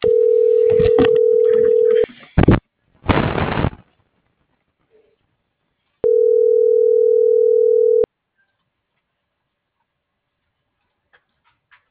Ringback tone (North America), 440 + 480 Hz.